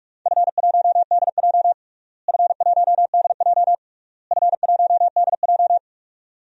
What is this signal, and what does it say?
Morse code, heard as a keyed tone, about 47 wpm, 700 Hz. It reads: F1DJ F1DJ F1DJ
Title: DXCC-100 Call Signs - 4 Characters - Repeated 3x - 45wpm